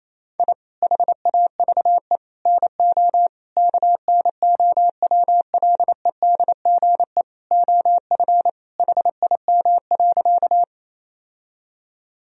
Morse code, heard as a keyed tone, about 28 wpm, 700 Hz. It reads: I HAVE NO KNOWLEDGE OF HIM.
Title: Sentences from Top 1000 Words - Review Part 3 - 28wpm